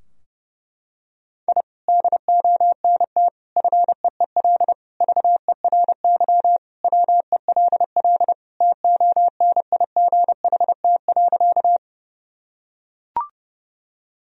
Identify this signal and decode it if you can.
Morse code, heard as a keyed tone, about 30 wpm, 700 Hz. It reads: I DONT FEEL VERY WELL TONIGHT.
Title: Sentences from Top 1000 Words - Review Part 3 - 30wpm